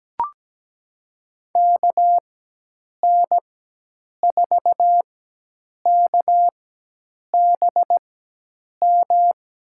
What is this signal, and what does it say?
Morse code, heard as a keyed tone, about 17 wpm, spaced as for 8 wpm, 700 Hz. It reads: KN4KBM
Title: USA Call Signs - 6 Characters - 4x Character Spacing - 17wpm